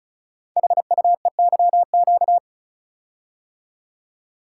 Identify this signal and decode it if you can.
Morse code, heard as a keyed tone, about 36 wpm, 700 Hz. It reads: HUEYQ
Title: Code Groups - 5 Letters - 35wpm